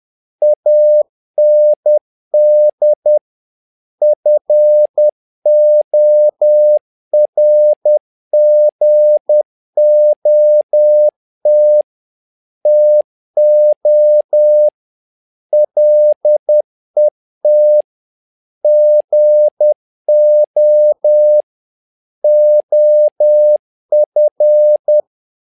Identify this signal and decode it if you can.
Morse code, heard as a keyed tone, about 10 wpm, 600 Hz. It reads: AND FORGOT TO LET GO OF
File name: learning center / tower trouble / tower trouble 10wpm